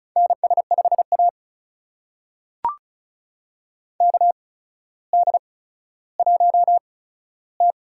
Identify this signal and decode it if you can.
Morse code, heard as a keyed tone, about 35 wpm, spaced as for 15 wpm, 700 Hz. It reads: NS5A KD1T
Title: USA Call Signs - 4 Characters - 8x Character Spacing - 35wpm